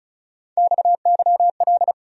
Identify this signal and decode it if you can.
Morse code, heard as a keyed tone, about 47 wpm, 700 Hz. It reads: XYL